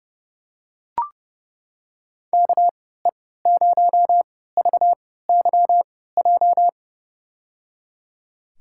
Morse code, heard as a keyed tone, about 30 wpm, spaced as for 17 wpm, 700 Hz. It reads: KE0VYJ
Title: USA Call Signs - 3x Character Spacing - 30wpm